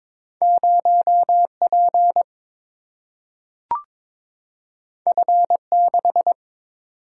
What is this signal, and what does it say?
Morse code, heard as a keyed tone, about 22 wpm, 700 Hz. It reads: 0P F6
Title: Single-Letter Single-Number 22wpm